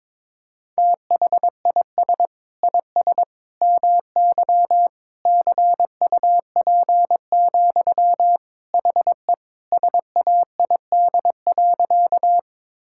Morse code, heard as a keyed tone, about 22 wpm, 700 Hz. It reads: THIS IS MY CUP, HE SAID.
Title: Sentences from Top 1600 Words 22wpm